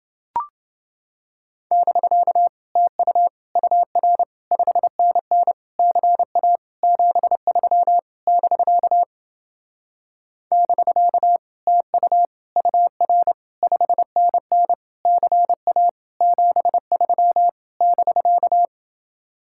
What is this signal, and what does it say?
Morse code, heard as a keyed tone, about 28 wpm, 700 Hz. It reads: <BK> TU UR 5NN CA 73 <BK> <BK> TU UR 5NN CA 73 <BK>